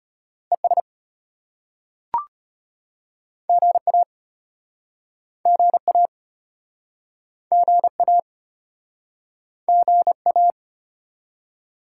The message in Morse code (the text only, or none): ES GA GA GA GA